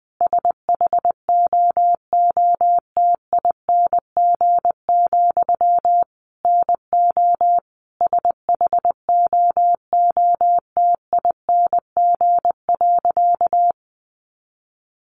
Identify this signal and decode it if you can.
Morse code, heard as a keyed tone, about 20 wpm, 700 Hz. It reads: SHOOTING, NO SHOOTING.
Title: A Sound Of Thunder 20wpm